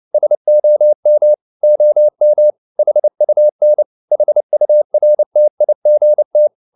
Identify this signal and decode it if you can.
Morse code, heard as a keyed tone, about 29 wpm, 600 Hz. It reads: SOM OM HUN HURTIGT
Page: CW med Gnister